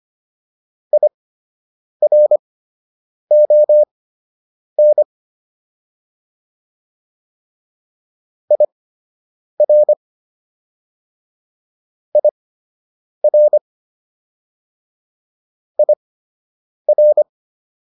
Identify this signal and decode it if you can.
Morse code, heard as a keyed tone, about 25 wpm, spaced as for 8 wpm, 600 Hz. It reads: IRON IR IR IR